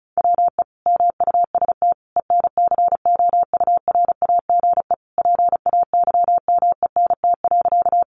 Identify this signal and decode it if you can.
Morse code, heard as a keyed tone, about 34 wpm, 700 Hz. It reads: WE MUST ENCOURAGE PAYMENT.
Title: Sentences from Top 1400 Words 35wpm